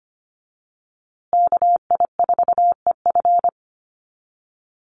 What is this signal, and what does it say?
Morse code, heard as a keyed tone, about 25 wpm, 700 Hz. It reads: KI4EF
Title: USA Call Signs Repeated 2x 25wpm